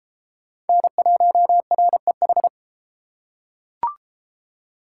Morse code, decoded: N1REH

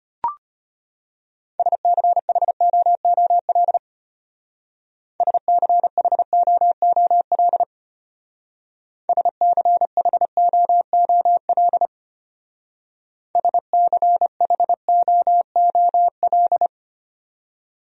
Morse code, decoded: SCHOOL SCHOOL SCHOOL SCHOOL

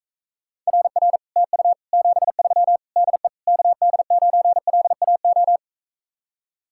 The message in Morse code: RR TU 73 DE KD9LAO

